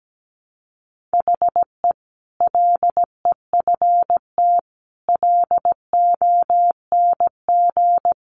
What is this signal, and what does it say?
Morse code, heard as a keyed tone, about 17 wpm, 700 Hz. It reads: HE LEFT LONG
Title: Sentences from Top 500 Words 17wpm